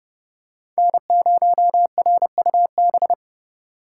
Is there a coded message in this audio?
Yes: Morse code from 0.8 to 3.1 s, heard as a keyed tone, about 29 wpm, 700 Hz. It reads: N0RUB